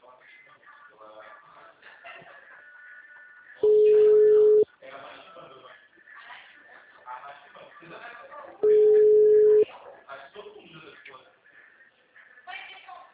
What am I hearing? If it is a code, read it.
Ringback tone (Europe, 425 Hz), with speech.
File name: out- -2002- - - .187